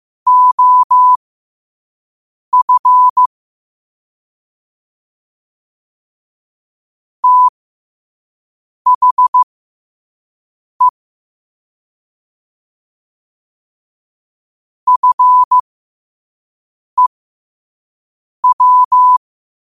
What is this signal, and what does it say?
Morse code, heard as a keyed tone, about 15 wpm, spaced as for 5 wpm, 1000 Hz. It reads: OF THE FEW
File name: qotdfiles / 5wpm